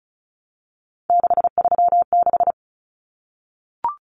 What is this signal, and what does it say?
Morse code, heard as a keyed tone, about 34 wpm, 700 Hz. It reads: 636
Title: Numbers - 3-Digit 35wpm